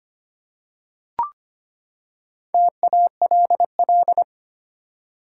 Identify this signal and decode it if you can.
Morse code, heard as a keyed tone, about 25 wpm, 700 Hz. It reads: TALL
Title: Four Letter Words - 25wpm